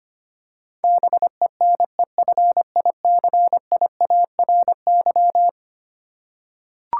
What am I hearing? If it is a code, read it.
Morse code, heard as a keyed tone, about 25 wpm, 700 Hz. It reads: BENEFICIARY
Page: morse-code-ninja-practice-25wpm